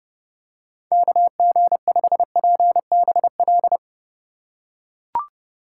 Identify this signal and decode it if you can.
Morse code, heard as a keyed tone, about 30 wpm, 700 Hz. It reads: KG5PBL